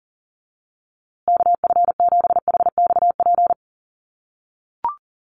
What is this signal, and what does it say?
Morse code, heard as a keyed tone, about 40 wpm, 700 Hz. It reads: KF7HXP
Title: USA Call Signs Repeated 3x 40wpm